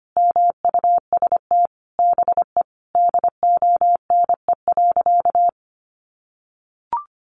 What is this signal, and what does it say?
Morse code, heard as a keyed tone, about 25 wpm, 700 Hz. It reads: MUST BE DONE.
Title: Sentences from Top 1100 Words 25wpm